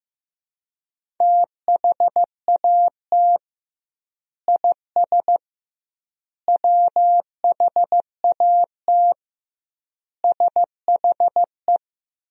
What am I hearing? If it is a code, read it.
Morse code, heard as a keyed tone, about 15 wpm, 700 Hz. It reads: THAT IS WHAT SHE
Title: 2x Word-Spacing - Sentences from Top 100 Words - 15wpm